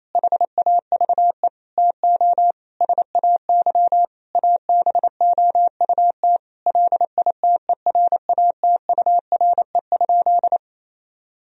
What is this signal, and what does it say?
Morse code, heard as a keyed tone, about 28 wpm, 700 Hz. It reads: HAVE TO SAY ABOUT LITERATURE?
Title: Sentences from Top 1000 Words - Review Part 3 - 28wpm